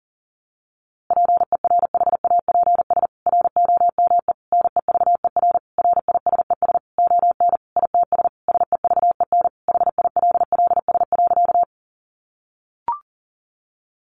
Morse code, heard as a keyed tone, about 40 wpm, 700 Hz. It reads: PERHAPS ROME NEVER RISES ON ITS SEVEN HILLS.
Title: A Sound Of Thunder 40wpm